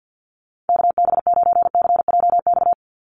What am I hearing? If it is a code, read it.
Morse code, heard as a keyed tone, about 50 wpm, 700 Hz. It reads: KB9CJX
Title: USA Call Signs 50wpm